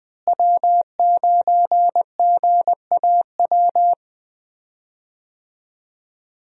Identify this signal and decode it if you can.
Morse code, heard as a keyed tone, about 20 wpm, 700 Hz. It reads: W9GAW
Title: USA Call Signs - 5 Characters - Repeated 2x - 20wpm